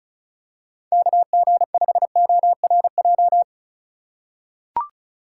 Morse code, heard as a keyed tone, about 36 wpm, 700 Hz. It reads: KG5ORJ
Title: USA Call Signs - 6 Characters - 4x Character Spacing - 35wpm